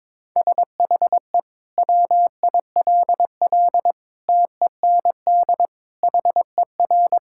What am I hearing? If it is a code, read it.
Morse code, heard as a keyed tone, about 21 wpm, 700 Hz. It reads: SHE WILL TEND HER
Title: Sentences from Top 1000 Words - Review Part 5 - 22wpm